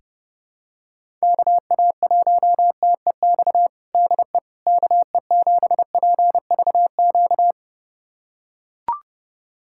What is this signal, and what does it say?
Morse code, heard as a keyed tone, about 30 wpm, 700 Hz. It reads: KA1TEX DE KE7PVQ
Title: USA Callsign DE Callsign 30wpm